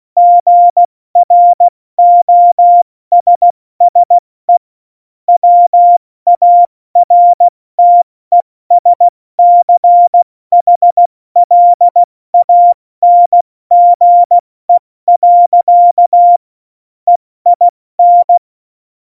Morse code, heard as a keyed tone, about 16 wpm, 700 Hz. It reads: GROSSE WARTESCHLANGE. EIN
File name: Morsekurs / Witze / Witz 13. November 2018, 70 F80